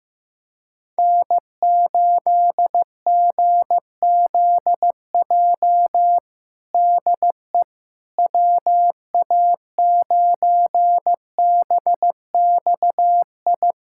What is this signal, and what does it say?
Morse code, heard as a keyed tone, about 15 wpm, 700 Hz. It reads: N8GZJ DE WA9BXI
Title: K1USN SST QSO Practice 15wpm